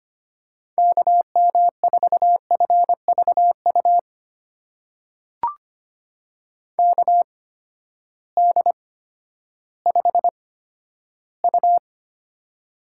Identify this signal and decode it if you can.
Morse code, heard as a keyed tone, about 25 wpm, spaced as for 12 wpm, 700 Hz. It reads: KM4FVU KD5U
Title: USA Call Signs - 6 Characters - 8x Character Spacing - 25wpm